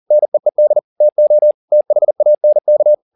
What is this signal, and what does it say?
Morse code, heard as a keyed tone, about 40 wpm, 600 Hz. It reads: NEED TO THANK